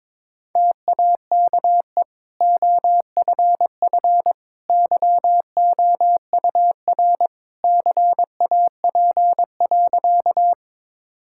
Morse code, heard as a keyed tone, about 22 wpm, 700 Hz. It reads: TAKE OFF YOUR CAP.
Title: Sentences from Top 1600 Words 22wpm